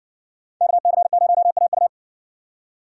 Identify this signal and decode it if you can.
Morse code, heard as a keyed tone, about 57 wpm, 700 Hz. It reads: KC0AU